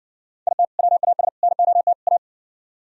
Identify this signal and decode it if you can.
Morse code, heard as a keyed tone, about 60 wpm, 700 Hz. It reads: IT WAS NOT A